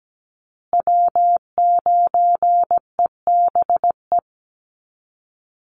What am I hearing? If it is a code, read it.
Morse code, heard as a keyed tone, about 17 wpm, 700 Hz. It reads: W9EBE
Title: Call Signs Repeated 3x - 17wpm